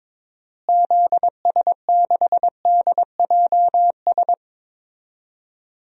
Morse code, heard as a keyed tone, about 22 wpm, 700 Hz. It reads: ZS6DJS
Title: DXCC-100 Call Signs - 6 Characters - Repeated 2x - 22wpm